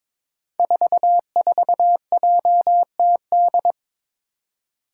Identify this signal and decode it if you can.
Morse code, heard as a keyed tone, about 22 wpm, 700 Hz. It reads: 44JTD
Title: Five Challenging Letter-Number Combinations 22wpm